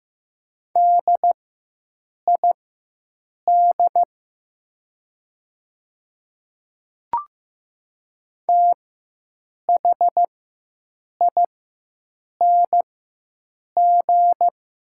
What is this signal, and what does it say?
Morse code, heard as a keyed tone, about 15 wpm, spaced as for 7 wpm, 700 Hz. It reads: DID THING
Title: 4x Character Spacing - Top 200 Words - 15wpm